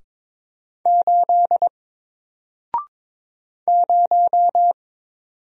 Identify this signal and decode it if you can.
Morse code, heard as a keyed tone, about 22 wpm, 700 Hz. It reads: 8 0
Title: Single Character (Letter:Number:Punctuation:Prosign) 22wpm